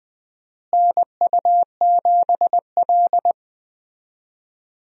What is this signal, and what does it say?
Morse code, heard as a keyed tone, about 20 wpm, 700 Hz. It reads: NU7L
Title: USA Call Signs - 4 Characters - Repeated 3x - 20wpm